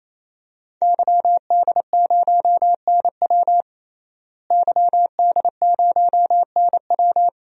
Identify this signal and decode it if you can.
Morse code, heard as a keyed tone, about 28 wpm, 690 Hz. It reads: YD0NW YD0NW
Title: DXCC-100 Call Signs - 5 Characters - Repeated 3x - 28wpm